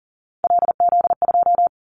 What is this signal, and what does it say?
Morse code, heard as a keyed tone, about 40 wpm, 700 Hz. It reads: LZ2
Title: Callsign Prefixes for Top 100 DXCC - 40wpm